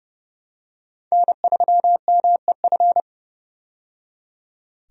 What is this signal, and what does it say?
Morse code, heard as a keyed tone, about 30 wpm, 700 Hz. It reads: N3MEF
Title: USA Call Signs - 5 Characters - 30wpm